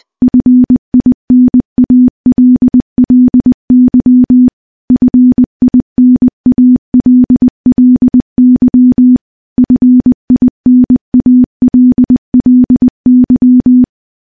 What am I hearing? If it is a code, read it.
Morse code, heard as a keyed tone, about 21 wpm, 250 Hz. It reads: FINALLY FINALLY FINALLY
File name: morse-training-20WPM